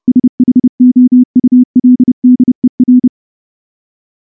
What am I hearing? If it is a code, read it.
Morse code, heard as a keyed tone, about 28 wpm, 250 Hz. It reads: SHOULDER
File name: morse / morse-training-30WPM